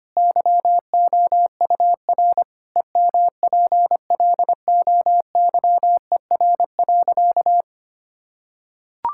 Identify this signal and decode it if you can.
Morse code, heard as a keyed tone, about 25 wpm, 700 Hz. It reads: YOUR EMPLOYER.
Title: Sentences from Top 1600 Words 25wpm